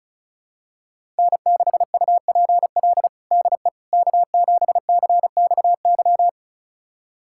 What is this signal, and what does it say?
Morse code, heard as a keyed tone, about 34 wpm, 700 Hz. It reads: N6UPL DE K7CXY
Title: USA Callsign DE Callsign - 5 Characters - 35wpm